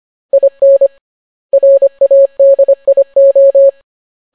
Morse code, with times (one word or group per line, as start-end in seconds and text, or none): 0.3-0.9 IN
1.5-3.7 RADIO